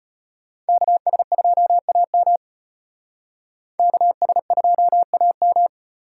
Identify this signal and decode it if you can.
Morse code, heard as a keyed tone, about 31 wpm, 700 Hz. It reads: KS2AM KS2AM